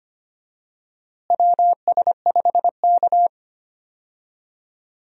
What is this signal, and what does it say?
Morse code, heard as a keyed tone, about 25 wpm, 700 Hz. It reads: WS5K
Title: USA Call Signs - 4 Characters - Repeated 2x - 25wpm